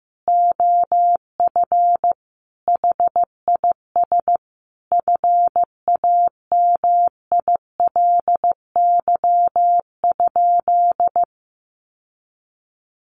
Morse code, heard as a keyed tone, about 15 wpm, 700 Hz. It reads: OF HIS FAMILY?
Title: Sentences from Top 1300 Words 15wpm